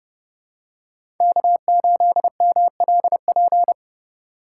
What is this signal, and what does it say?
Morse code, heard as a keyed tone, about 31 wpm, 700 Hz. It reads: K8MLP
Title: USA Call Signs - 5 Characters - 3x Character Spacing - 30wpm